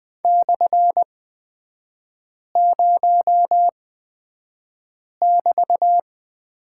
Morse code, heard as a keyed tone, about 20 wpm, spaced as for 5 wpm, 700 Hz. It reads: /0<BT>
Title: 250 Lesson 40e - BK - Three Characters 20wpm